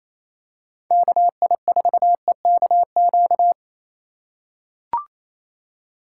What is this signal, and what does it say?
Morse code, heard as a keyed tone, about 27 wpm, 700 Hz. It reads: KI4EKQ
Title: USA Call Signs - 4x Character Spacing - 28wpm